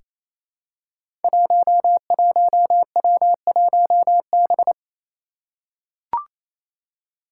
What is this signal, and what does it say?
Morse code, heard as a keyed tone, about 28 wpm, 700 Hz. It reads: 11W1B
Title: Five Challenging Letter-Number Combinations 28wpm